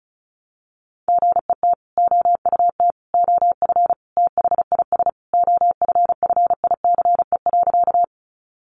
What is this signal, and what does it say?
Morse code, heard as a keyed tone, about 34 wpm, 700 Hz. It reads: GET OUT OF THIS OFFICE.
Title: Sentences from Top 1000 Words 35wpm